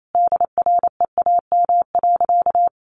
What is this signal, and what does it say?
Morse code, heard as a keyed tone, about 28 wpm, 700 Hz. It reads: DREAM.
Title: Sentences from Top 1500 Words 28wpm